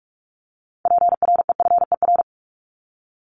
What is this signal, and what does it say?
Morse code, heard as a keyed tone, about 44 wpm, 700 Hz. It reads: PREFER
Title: Top 1001-1250 Words 45wpm